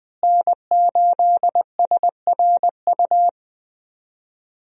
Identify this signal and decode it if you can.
Morse code, heard as a keyed tone, about 20 wpm, 700 Hz. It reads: N8SRU
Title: USA Call Signs - 5 Characters - Repeated 2x - 20wpm